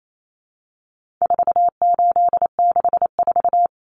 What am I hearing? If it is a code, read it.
Morse code, heard as a keyed tone, about 28 wpm, 700 Hz. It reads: 4864
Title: Numbers - 4-Digit 28wpm